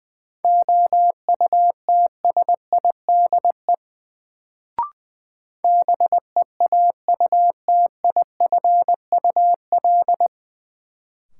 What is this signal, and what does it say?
Morse code, heard as a keyed tone, about 21 wpm, 700 Hz. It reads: OUTSIDE BEAUTIFUL
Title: Sentences from Top 900 Words - New Word Drill - 20wpm